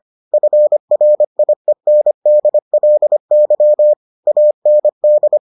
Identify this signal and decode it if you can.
Morse code, heard as a keyed tone, about 25 wpm, 600 Hz. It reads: FRIENDLY AND